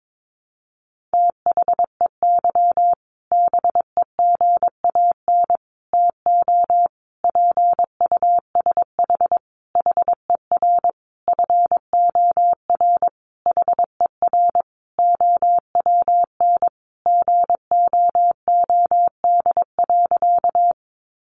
Morse code, heard as a keyed tone, about 22 wpm, 700 Hz. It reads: THEY BEGAN TO PUSH HER FOR HER OWN GOOD.
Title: Sentences from Top 1300 Words 22wpm